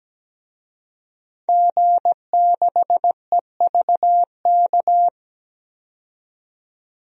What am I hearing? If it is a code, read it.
Morse code, heard as a keyed tone, about 17 wpm, 700 Hz. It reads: G6EVK
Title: DXCC-100 Call Signs - 5 Characters - 17wpm